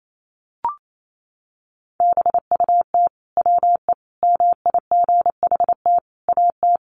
Morse code, heard as a keyed tone, about 28 wpm, 700 Hz. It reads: BUT WE MIGHT AT
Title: Sentences from Top 1300 Words 28wpm